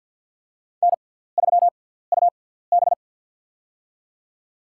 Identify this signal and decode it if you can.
Morse code, heard as a keyed tone, about 53 wpm, spaced as for 17 wpm, 700 Hz. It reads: N3UB
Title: USA Call Signs - 4 Characters - 6x Character Spacing - 50wpm